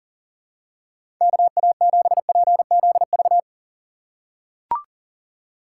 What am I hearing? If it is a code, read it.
Morse code, heard as a keyed tone, about 38 wpm, 700 Hz. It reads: KA7PZV
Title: USA Call Signs - 6 Characters - 4x Character Spacing - 40wpm